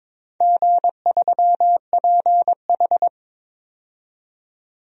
Morse code, heard as a keyed tone, about 22 wpm, 700 Hz. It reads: G3PH
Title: DXCC-100 Call Signs - 4 Characters - 22wpm